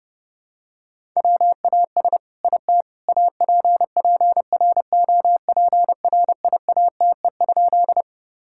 Morse code, heard as a keyed tone, about 30 wpm, 700 Hz. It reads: WAS IT APPROPRIATE?
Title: Sentences from Top 1000 Words - Review Part 5 - 30wpm